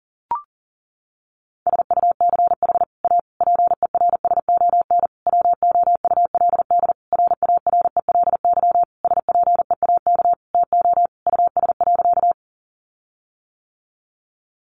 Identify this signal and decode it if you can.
Morse code, heard as a keyed tone, about 37 wpm, 700 Hz. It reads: SUCH A PERSON WOULD RARELY SPEAK TO US.
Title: Sentences from Top 1400 Words 40wpm